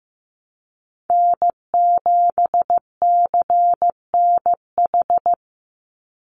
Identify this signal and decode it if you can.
Morse code, heard as a keyed tone, about 15 wpm, 700 Hz. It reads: N7CNH
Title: NA SOTA Chaser Callsigns 15wpm